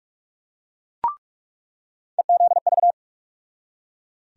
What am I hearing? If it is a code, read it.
Morse code, heard as a keyed tone, about 41 wpm, 700 Hz. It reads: EZV
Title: USA Call Sign Postfixes - 3 Letters 45wpm